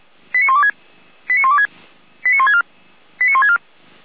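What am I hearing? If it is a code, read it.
Five-tone selective call (ZVEI), heard as quick runs of five tones, tones of about 70 ms. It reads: 89126 89126 89164 89164
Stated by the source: ZVEI five-tone selective call 89338, received by SDR.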